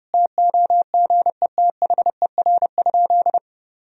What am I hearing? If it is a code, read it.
Morse code, heard as a keyed tone, about 30 wpm, 700 Hz. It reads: TOGETHER?
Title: Sentences from Top 1000 Words - Review Part 2 - 30wpm